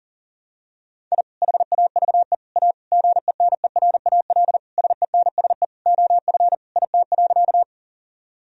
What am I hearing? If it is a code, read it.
Morse code, heard as a keyed tone, about 40 wpm, 700 Hz. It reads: I HAVE A GENERAL SENSE OF IT.
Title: Sentences from Top 1000 Words - Review Part 4 - 40wpm